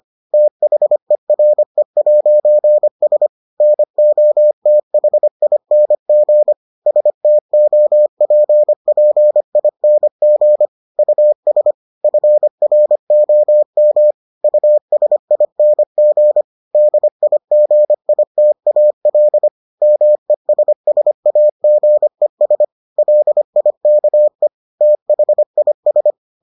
Morse code, heard as a keyed tone, about 25 wpm, 600 Hz. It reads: THERE'S NOTHING STOPPING US FROM USING DIGITAL MESSAGES LIKE THIS